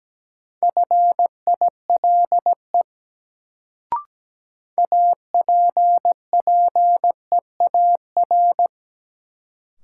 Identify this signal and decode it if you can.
Morse code, heard as a keyed tone, about 17 wpm, 700 Hz. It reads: FILE APPEAR
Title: Sentences from Top 1200 Words - New Word Drill Alt - 17wpm